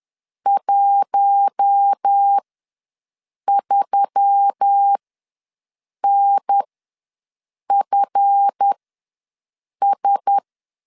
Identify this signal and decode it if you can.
Morse code, heard as a keyed tone, about 10 wpm, spaced as for 6 wpm, 800 Hz. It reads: 13NFS
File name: LEZIONI / 7 GF9Z / 7 GF ws 10wpm 3918